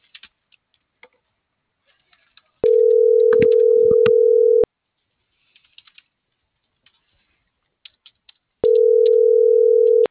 Ringback tone (North America), 440 + 480 Hz.